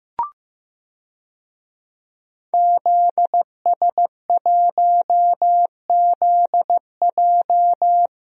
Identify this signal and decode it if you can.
Morse code, heard as a keyed tone, about 15 wpm, 700 Hz. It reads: ZS1ZJ
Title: DXCC-100 Call Signs - 5 Characters - Repeated 3x - 15wpm